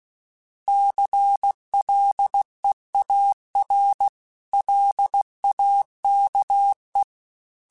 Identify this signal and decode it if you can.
Morse code, heard as a keyed tone, about 16 wpm, 800 Hz. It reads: CLEAR LAKE
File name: fromSMs / FromSMs mp3s / Cities@15